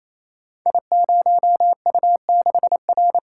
Morse code, heard as a keyed tone, about 27 wpm, 700 Hz. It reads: I0U6R
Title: Code Groups - 5 Letter:Numbers - 28wpm